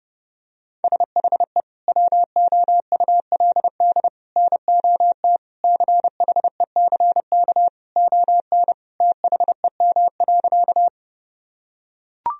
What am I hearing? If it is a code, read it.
Morse code, heard as a keyed tone, about 30 wpm, 700 Hz. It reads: SHE WOULD NOT CHECK ON THEM.
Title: Sentences from Top 1000 Words - Review Part 3 - 30wpm